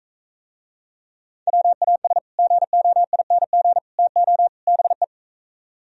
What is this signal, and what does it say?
Morse code, heard as a keyed tone, about 43 wpm, 700 Hz. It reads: WAS GOING TO BE